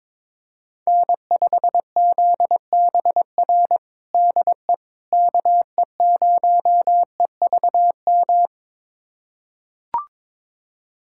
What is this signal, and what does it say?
Morse code, heard as a keyed tone, about 21 wpm, 700 Hz. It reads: N5ZBR DE KE0EVM
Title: USA Callsign DE Callsign 22wpm